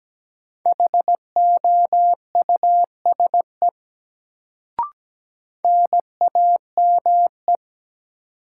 Morse code, heard as a keyed tone, about 17 wpm, 700 Hz. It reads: HOUSE NAME
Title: Sentences from Top 200 Words - New Word Drill - 17wpm